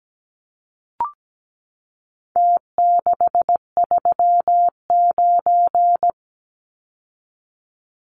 Morse code, heard as a keyed tone, about 17 wpm, 700 Hz. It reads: T639
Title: Numbers - Contest Style - 4-Digit 17wpm